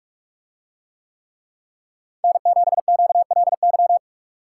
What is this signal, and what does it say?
Morse code, heard as a keyed tone, about 44 wpm, 700 Hz. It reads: N7QLY